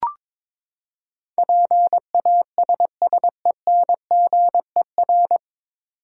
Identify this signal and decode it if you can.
Morse code, heard as a keyed tone, about 22 wpm, 700 Hz. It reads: PASSENGER
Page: Morse Code Ninja practice 22wpm